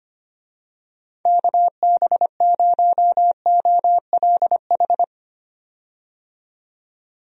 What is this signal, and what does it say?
Morse code, heard as a keyed tone, about 25 wpm, 700 Hz. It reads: KB0OLH
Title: USA Call Signs - 6 Characters - 25wpm